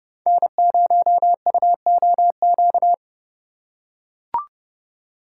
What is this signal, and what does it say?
Morse code, heard as a keyed tone, about 30 wpm, 700 Hz. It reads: N0UOQ
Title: USA Call Signs - 5 Characters - 4x Character Spacing - 30wpm